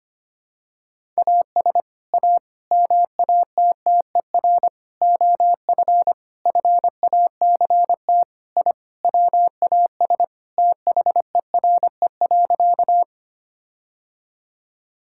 Morse code, heard as a keyed tone, about 25 wpm, 700 Hz. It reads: AS A MATTER OF FACT I WAS THERE.
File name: practice / 25wpm / Sentences with 4-Word Phrases - 25wpm